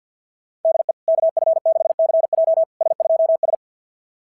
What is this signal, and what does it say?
Morse code, heard as a keyed tone, about 46 wpm, 660 Hz. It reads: DE KF6CJ S2S
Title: SOTA QSO Practice 50wpm